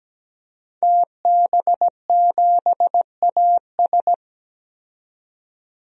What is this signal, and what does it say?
Morse code, heard as a keyed tone, about 17 wpm, 700 Hz. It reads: TB7AS
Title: DXCC-100 Call Signs - 5 Characters - 17wpm